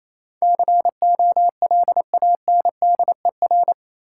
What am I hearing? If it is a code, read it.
Morse code, heard as a keyed tone, about 27 wpm, 690 Hz. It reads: COLANDER
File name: Easy Words - Speed-Racing with Hints - 22wpm